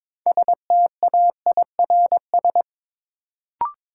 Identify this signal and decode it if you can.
Morse code, heard as a keyed tone, about 22 wpm, 700 Hz. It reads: STAIRS
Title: Sets of 5 Words Encore - New Word Drill - 22wpm